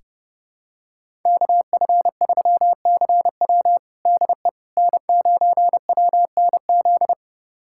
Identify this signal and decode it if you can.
Morse code, heard as a keyed tone, about 29 wpm, 700 Hz. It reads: KF3CW DE N9WNZ
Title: USA Callsign DE Callsign - 5 Characters - 30wpm